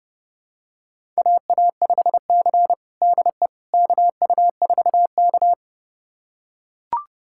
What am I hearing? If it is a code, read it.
Morse code, heard as a keyed tone, about 30 wpm, 700 Hz. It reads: AA5C DE KU4K
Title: USA Callsign DE Callsign - 4 Characters - 30wpm